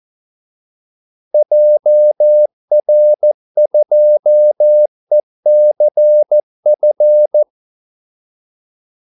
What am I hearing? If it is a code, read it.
Morse code, heard as a keyed tone, about 14 wpm, 600 Hz. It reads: JR2ECF
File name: Callsigns-14wpm